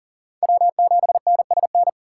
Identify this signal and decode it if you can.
Morse code, heard as a keyed tone, about 40 wpm, 700 Hz. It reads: W7NSN
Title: USA Call Signs - 6x Character Spacing - 40wpm